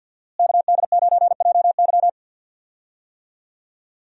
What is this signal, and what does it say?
Morse code, heard as a keyed tone, about 48 wpm, 700 Hz. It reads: KD9JY
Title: USA Call Signs - 5 Characters - Repeated 2x - 50wpm